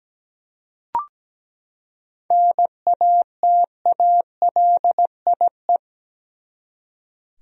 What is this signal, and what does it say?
Morse code, heard as a keyed tone, about 17 wpm, 700 Hz. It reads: NATALIE
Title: USA Common Names - 17wpm